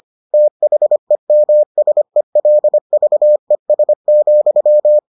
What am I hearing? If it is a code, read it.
Morse code, heard as a keyed tone, about 25 wpm, 600 Hz. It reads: THEMSELVES,